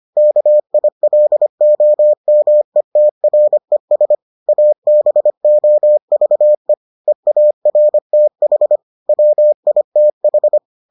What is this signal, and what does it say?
Morse code, heard as a keyed tone, about 25 wpm, 600 Hz. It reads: KILOMETRES ABOVE EARTH WITH